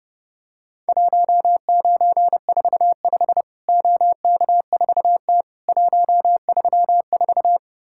Morse code, heard as a keyed tone, about 30 wpm, 700 Hz. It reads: 1945 OK4T 134
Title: Worked All Europe DX Contest - QTC Exchange - 30wpm